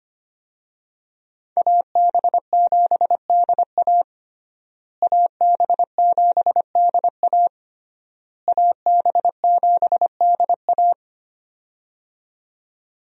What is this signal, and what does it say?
Morse code, heard as a keyed tone, about 25 wpm, 700 Hz. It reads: AB7DA AB7DA AB7DA
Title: USA Call Signs - 5 Characters - Repeated 3x - 25wpm